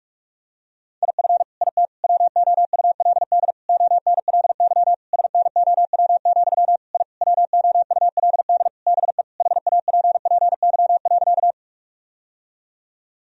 Morse code, heard as a keyed tone, about 46 wpm, 700 Hz. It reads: IF IT WOULD ONLY SNOW, I WOULD BE HAPPY.
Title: Sentences from Top 1000 Words - Review Part 3 - 45wpm